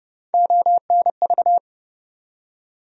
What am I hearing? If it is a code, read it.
Morse code, heard as a keyed tone, about 30 wpm, 700 Hz. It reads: ONV
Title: USA Call Sign Postfixes - 3 Letters 30wpm